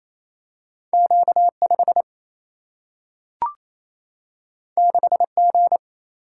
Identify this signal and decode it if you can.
Morse code, heard as a keyed tone, about 28 wpm, 700 Hz. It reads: Q 5 6 G
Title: Single-Letter Single-Number 28wpm